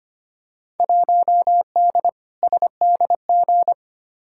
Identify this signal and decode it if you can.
Morse code, heard as a keyed tone, about 25 wpm, 700 Hz. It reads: 1D SDG
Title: Field Day Exchanges - 25wpm